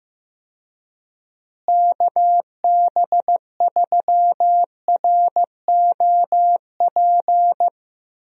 Morse code, heard as a keyed tone, about 15 wpm, 700 Hz. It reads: KB3ROP
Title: USA Call Signs - 6 Characters - Repeated 3x - 15wpm